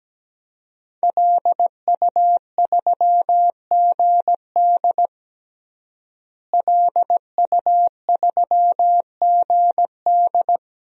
Morse code, heard as a keyed tone, about 17 wpm, 700 Hz. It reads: LU3GD LU3GD
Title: DXCC-100 Call Signs - 5 Characters - Repeated 2x - 17wpm